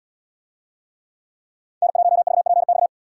Morse code, heard as a keyed tone, about 77 wpm, 700 Hz. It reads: N1VPX